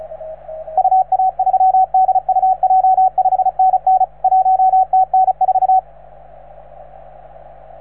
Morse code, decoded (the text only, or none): UA3DUJ 5NN 1TN4